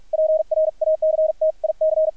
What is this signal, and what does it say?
Morse code, heard as a keyed tone, about 50 wpm, spaced as for 42 wpm, 630 Hz, with radio noise. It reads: PRACTIC
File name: archived / CRSnet